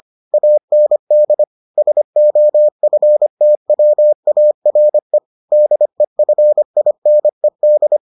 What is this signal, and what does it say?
Morse code, heard as a keyed tone, about 25 wpm, 600 Hz. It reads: AND SOFTWARE DEFINED